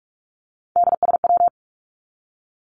Morse code, heard as a keyed tone, about 44 wpm, 700 Hz. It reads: DSW
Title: Abbreviations 45wpm